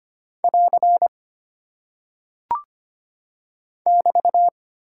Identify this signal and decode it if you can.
Morse code, heard as a keyed tone, about 25 wpm, 700 Hz. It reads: <AR> <BT>
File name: practice / 25wpm / Single Punctuation-Prosign - 25wpm